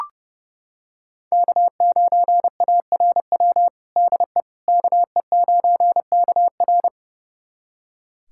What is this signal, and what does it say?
Morse code, heard as a keyed tone, about 30 wpm, 700 Hz. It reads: K9ARW DE KE9KR
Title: USA Callsign DE Callsign - 5 Characters - 30wpm